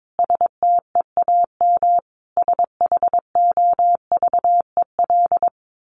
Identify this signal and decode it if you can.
Morse code, heard as a keyed tone, about 22 wpm, 700 Hz. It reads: STEAM SHOVEL
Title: A Sound Of Thunder 22wpm